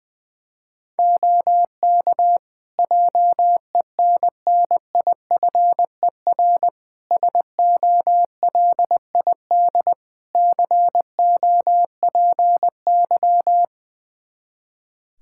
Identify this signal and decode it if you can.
Morse code, heard as a keyed tone, about 20 wpm, 700 Hz. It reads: OK JENNIFER SOLID COPY
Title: Practice QSO - Exchange 3 - More Info - 20wpm